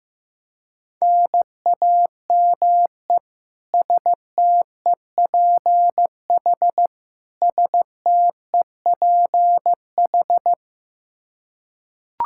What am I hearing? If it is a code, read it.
Morse code, heard as a keyed tone, about 15 wpm, 700 Hz. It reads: NAME STEPH STEPH
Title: Practice QSO - Exchange 1 - Beginning - 15wpm